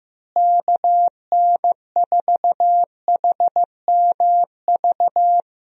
Morse code, heard as a keyed tone, about 15 wpm, 700 Hz. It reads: KN4HMV
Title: USA Call Signs - 6 Characters - Repeated 3x - 15wpm